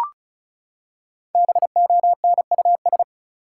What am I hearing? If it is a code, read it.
Morse code, heard as a keyed tone, about 34 wpm, 700 Hz. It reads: BONUS
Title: Five Letter Words - 35wpm